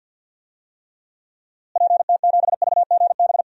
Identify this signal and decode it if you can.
Morse code, heard as a keyed tone, about 51 wpm, 700 Hz. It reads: PT7VGB
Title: DXCC-100 Call Signs - 6 Characters - Repeated 2x - 50wpm